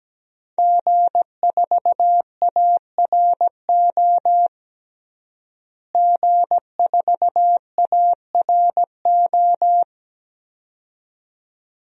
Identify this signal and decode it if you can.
Morse code, heard as a keyed tone, about 17 wpm, 700 Hz. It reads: G4ARO G4ARO
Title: DXCC-100 Call Signs - 5 Characters - Repeated 2x - 17wpm